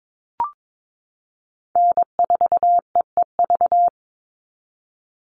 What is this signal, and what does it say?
Morse code, heard as a keyed tone, about 22 wpm, 700 Hz. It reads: N4EEV
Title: Call Signs - 22wpm